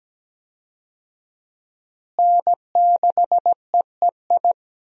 Morse code, heard as a keyed tone, about 17 wpm, 700 Hz. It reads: N6EEI
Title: USA Call Signs - 5 Characters - Repeated 2x - 17wpm